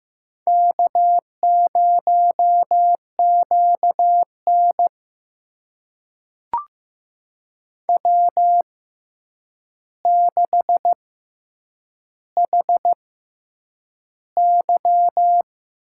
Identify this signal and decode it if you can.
Morse code, heard as a keyed tone, about 15 wpm, spaced as for 8 wpm, 700 Hz. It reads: K0QN W6HY
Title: USA Call Signs - 4 Characters - 6x Character Spacing - 15wpm